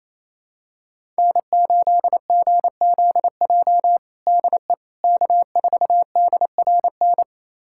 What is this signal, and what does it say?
Morse code, heard as a keyed tone, about 27 wpm, 700 Hz. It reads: N8GZJ DE K4DRN
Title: USA Callsign DE Callsign 28wpm